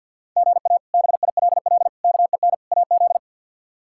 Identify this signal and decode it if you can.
Morse code, heard as a keyed tone, about 52 wpm, 700 Hz. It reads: GA BILL KEN AZ
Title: K1USN SST QSO Practice - 50wpm